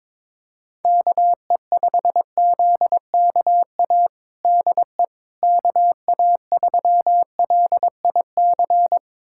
Morse code, heard as a keyed tone, about 22 wpm, 700 Hz. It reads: KE5ZKA DE KA3LIC
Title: USA Callsign DE Callsign 22wpm